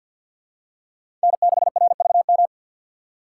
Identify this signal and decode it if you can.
Morse code, heard as a keyed tone, about 53 wpm, 690 Hz. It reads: N6RVM
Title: USA Call Signs - 2x Character Spacing - 50wpm